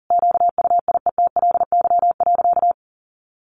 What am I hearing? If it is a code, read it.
Morse code, heard as a keyed tone, about 40 wpm, 700 Hz. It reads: QUIETLY.
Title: A Sound Of Thunder 40wpm